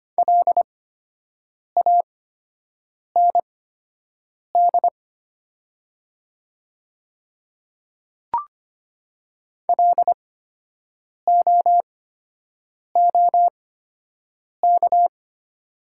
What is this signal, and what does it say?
Morse code, heard as a keyed tone, about 25 wpm, spaced as for 7 wpm, 700 Hz. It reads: LAND LOOK
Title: 8x Character Spacing - Top 200 Words - 25wpm